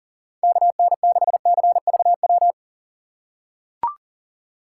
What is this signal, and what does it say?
Morse code, heard as a keyed tone, about 38 wpm, 700 Hz. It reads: KN6CVW
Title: USA Call Signs - 6 Characters - 3x Character Spacing - 40wpm